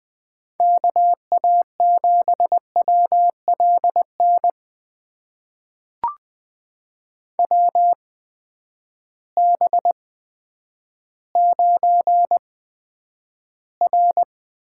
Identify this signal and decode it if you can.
Morse code, heard as a keyed tone, about 21 wpm, spaced as for 10 wpm, 700 Hz. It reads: KA7WLN WB9R
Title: USA Call Signs - 6 Characters - 8x Character Spacing - 20wpm